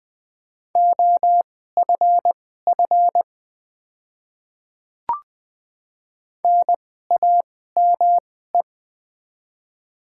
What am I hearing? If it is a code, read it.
Morse code, heard as a keyed tone, about 20 wpm, spaced as for 15 wpm, 700 Hz. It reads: OFF NAME
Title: 2x Character Spacing - Top 200 Words - 20wpm